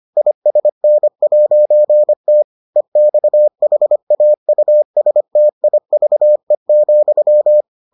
Morse code, heard as a keyed tone, about 25 wpm, 600 Hz. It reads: ISN'T EXHAUSTIVE,